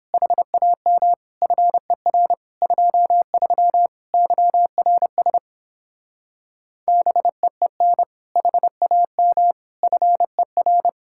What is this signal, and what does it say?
Morse code, heard as a keyed tone, about 26 wpm, 700 Hz. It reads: HAM FER 23 YRS BEEN HAM FER
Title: Speed-Racing Common QSO Phrases Encore - 22wpm